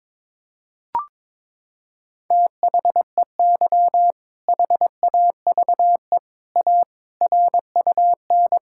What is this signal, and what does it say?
Morse code, heard as a keyed tone, about 22 wpm, 700 Hz. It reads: THEY HAVE A RUN
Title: Sentences from Top 1000 Words - Review Part 5 - 22wpm